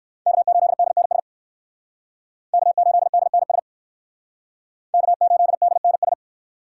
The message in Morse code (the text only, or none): K8DNS K8DNS K8DNS